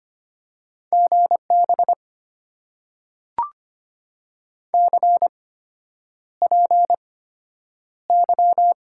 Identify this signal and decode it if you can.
Morse code, heard as a keyed tone, about 25 wpm, spaced as for 9 wpm, 700 Hz. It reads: GB CPY
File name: Top 90 QSO Elements - 8x Character Spacing - 25wpm